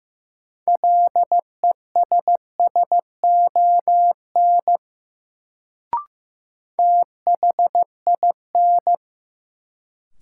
Morse code, heard as a keyed tone, about 15 wpm, 700 Hz. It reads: LESSON THIN
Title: Sentences from Top 1200 Words - New Word Drill - 15wpm